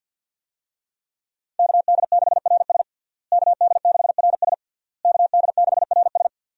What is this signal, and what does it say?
Morse code, heard as a keyed tone, about 49 wpm, 700 Hz. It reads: KD6RS KD6RS KD6RS
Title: USA Call Signs - 5 Characters - Repeated 3x - 50wpm